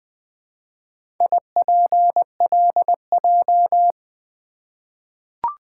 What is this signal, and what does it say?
Morse code, heard as a keyed tone, about 20 wpm, 700 Hz. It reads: IPLJ
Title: Four Letter-Number Combinations - 20wpm